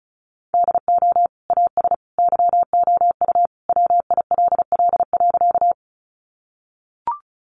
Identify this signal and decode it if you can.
Morse code, heard as a keyed tone, about 34 wpm, 700 Hz. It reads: DO AS YOU WILL.
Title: Sentences from Top 100 Words 35wpm